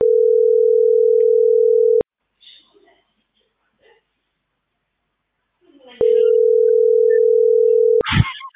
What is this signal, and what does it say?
Ringback tone (North America), 440 + 480 Hz.